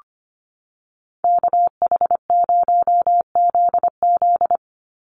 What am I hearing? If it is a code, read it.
Morse code, heard as a keyed tone, about 25 wpm, 700 Hz. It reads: KH0ZZ
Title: Call Signs - 25wpm